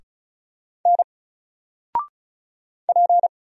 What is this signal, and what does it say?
Morse code, heard as a keyed tone, about 35 wpm, 700 Hz. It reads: N P
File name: Single-Letter-Number-35wpm